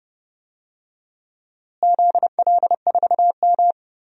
Morse code, heard as a keyed tone, about 30 wpm, 700 Hz. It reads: ZL4M
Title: DXCC-100 Call Signs - 4 Characters - Repeated 2x - 30wpm